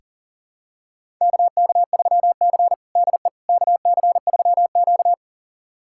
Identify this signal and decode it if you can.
Morse code, heard as a keyed tone, about 37 wpm, 700 Hz. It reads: KK3C DE KC3Q
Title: USA Callsign DE Callsign - 4 Characters - 40wpm